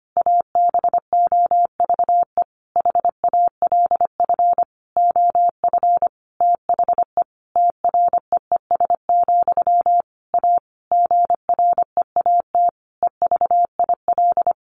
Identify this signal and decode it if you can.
Morse code, heard as a keyed tone, about 25 wpm, 700 Hz. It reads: ABOVE HALF OF THE TREES, A GREAT EVIL
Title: A Sound Of Thunder 25wpm